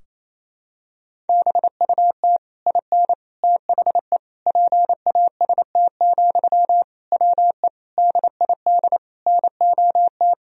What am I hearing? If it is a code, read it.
Morse code, heard as a keyed tone, about 28 wpm, 700 Hz. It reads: BUT IN THE PAST, WE DID NOT
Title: Sentences from Top 1000 Words - Review Part 3 - 28wpm